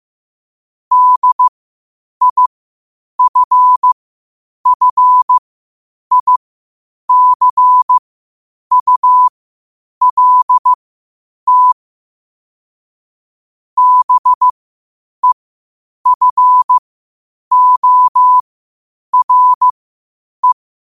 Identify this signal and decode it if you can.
Morse code, heard as a keyed tone, about 15 wpm, spaced as for 8 wpm, 1000 Hz. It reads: DIFFICULT BEFORE
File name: qotdfiles / 8wpm